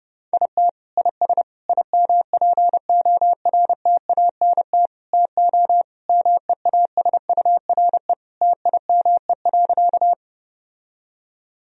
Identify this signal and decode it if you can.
Morse code, heard as a keyed tone, about 32 wpm, 700 Hz. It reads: IT IS IMPORTANT TO MEASURE TIME.
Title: Sentences from Top 1000 Words - Review Part 2 - 30wpm